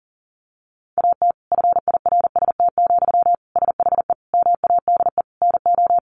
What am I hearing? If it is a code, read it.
Morse code, heard as a keyed tone, about 40 wpm, 700 Hz. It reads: AT FIRST, SHE MADE NO
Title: Sentences from Top 1400 Words 40wpm